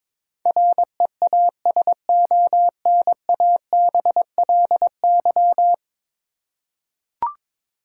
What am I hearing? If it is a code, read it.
Morse code, heard as a keyed tone, about 21 wpm, 700 Hz. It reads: REASONABLY
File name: 10 Letter Words - First 3 Letters Given - 22wpm